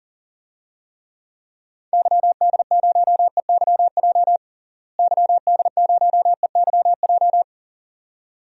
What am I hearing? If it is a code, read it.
Morse code, heard as a keyed tone, about 40 wpm, 700 Hz. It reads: YD0EYJ YD0EYJ
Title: DXCC-100 Call Signs - 6 Characters - Repeated 2x - 40wpm